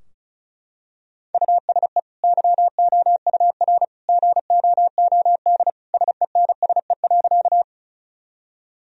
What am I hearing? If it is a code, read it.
Morse code, heard as a keyed tone, about 36 wpm, 700 Hz. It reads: USE YOUR GOOD SENSE.
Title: Sentences from Top 1000 Words - Review Part 4 - 35wpm